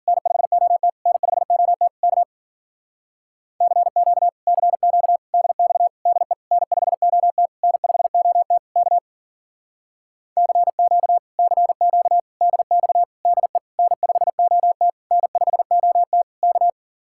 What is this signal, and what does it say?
Morse code, heard as a keyed tone, about 51 wpm, 700 Hz. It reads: N5OT N5OT K CQ CQ DX DE N5OT N5OT K CQ CQ DX DE N5OT N5OT K